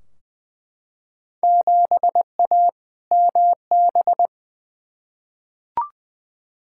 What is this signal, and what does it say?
Morse code, heard as a keyed tone, about 20 wpm, 700 Hz. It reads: 7A MB
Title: Field Day Exchanges Encore - 20wpm